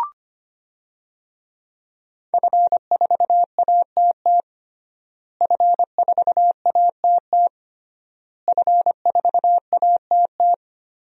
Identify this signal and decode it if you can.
Morse code, heard as a keyed tone, about 25 wpm, 700 Hz. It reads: F4ATT F4ATT F4ATT
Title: DXCC-100 Call Signs - 5 Characters - Repeated 3x - 25wpm